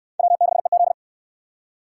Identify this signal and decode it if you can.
Morse code, heard as a keyed tone, about 68 wpm, 700 Hz. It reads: W6EZ